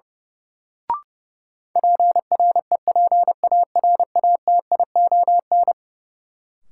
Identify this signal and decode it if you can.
Morse code, heard as a keyed tone, about 30 wpm, 700 Hz. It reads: PREPARATION
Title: Sentences from Top 1400 Words - New Word Drill - 30wpm